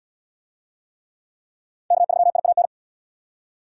Morse code, heard as a keyed tone, about 77 wpm, 700 Hz. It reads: K3IAN